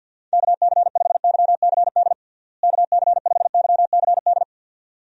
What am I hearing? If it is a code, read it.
Morse code, heard as a keyed tone, about 50 wpm, 700 Hz. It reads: KC5YCD KC5YCD
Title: USA Call Signs - 6 Characters - Repeated 2x - 50wpm